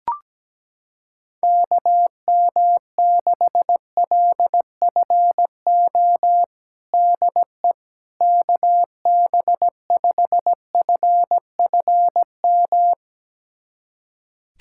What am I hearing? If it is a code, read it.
Morse code, heard as a keyed tone, about 17 wpm, 700 Hz. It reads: KM6LFO DE KB5FFM
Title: USA Callsign DE Callsign - 6 Characters - 17wpm